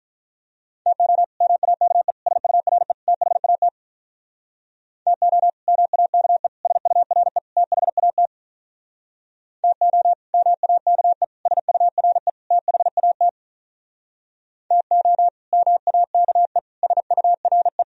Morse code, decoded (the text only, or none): TO MAKE SURE THAT TO MAKE SURE THAT TO MAKE SURE THAT TO MAKE SURE